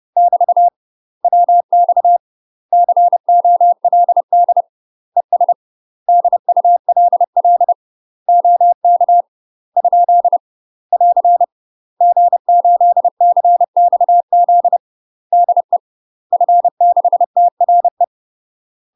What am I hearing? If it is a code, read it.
Morse code, heard as a keyed tone, about 30 wpm, 700 Hz. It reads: <BT> WX COLD ES DULL OK ? <AR> G8CXZ DE F6TRE
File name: UK MASTER QSOs TRAINING / 30WPM / UK MASTER QSOs 0046